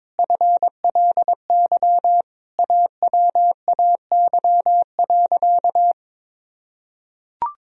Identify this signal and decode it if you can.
Morse code, heard as a keyed tone, about 22 wpm, 700 Hz. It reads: FLY AWAY.
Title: Sentences from Top 1000 Words - Review Part 2 - 22wpm